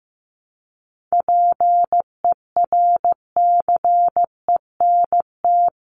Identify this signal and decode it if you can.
Morse code, heard as a keyed tone, about 14 wpm, 700 Hz. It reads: PERCENT
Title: Sets of 3 Words 15wpm